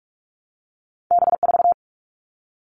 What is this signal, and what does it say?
Morse code, heard as a keyed tone, about 46 wpm, 700 Hz. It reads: B4
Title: All Cut Numbers - 2-Digit 45wpm